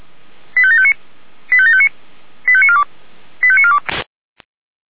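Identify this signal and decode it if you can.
Five-tone selective call (ZVEI), heard as quick runs of five tones, tones of about 70 ms. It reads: 85759 85759 85942 85942
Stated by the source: ZVEI five-tone selective call received by SDR.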